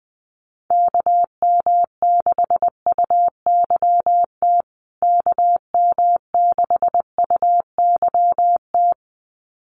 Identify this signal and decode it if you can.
Morse code, heard as a keyed tone, about 20 wpm, 700 Hz. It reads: KM6UYT KM6UYT
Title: USA Call Signs Repeated 2x 20wpm